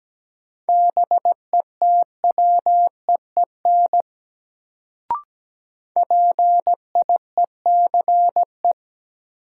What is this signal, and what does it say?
Morse code, heard as a keyed tone, about 17 wpm, 700 Hz. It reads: BETWEEN PIECE
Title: Sentences from Top 400 Words - New Word Drill - 17wpm